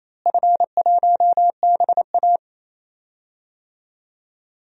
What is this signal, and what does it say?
Morse code, heard as a keyed tone, about 28 wpm, 700 Hz. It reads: F1BA
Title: DXCC-100 Call Signs - 4 Characters - Repeated 2x - 28wpm